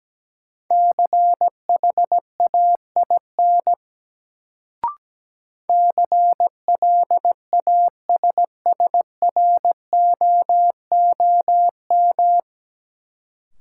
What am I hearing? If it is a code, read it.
Morse code, heard as a keyed tone, about 17 wpm, 700 Hz. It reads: CHAIN CLASSROOM
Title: Sentences from Top 1500 Words - New Word Drill Alt - 17wpm